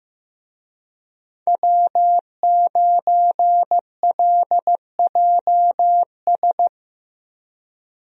Morse code, heard as a keyed tone, about 15 wpm, 700 Hz. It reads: W9LJS
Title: USA Call Signs - 5 Characters - Repeated 3x - 15wpm